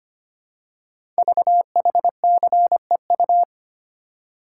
Morse code, heard as a keyed tone, about 25 wpm, 700 Hz. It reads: VHCEU
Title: Code Groups - 5 Letters - 25wpm